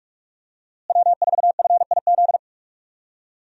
Morse code, heard as a keyed tone, about 44 wpm, 700 Hz. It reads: W4FIZ
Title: USA Call Signs - 5 Characters - Repeated 2x - 45wpm